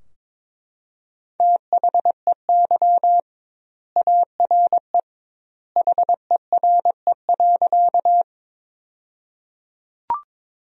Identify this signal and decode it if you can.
Morse code, heard as a keyed tone, about 22 wpm, 700 Hz. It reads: THEY ARE HERE.
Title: 2x Word-Spacing - Sentences from Top 200 Words - 22wpm